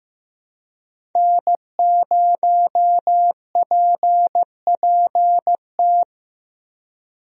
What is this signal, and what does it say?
Morse code, heard as a keyed tone, about 15 wpm, 700 Hz. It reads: N0PPT
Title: USA Call Signs - 5 Characters - Repeated 3x - 15wpm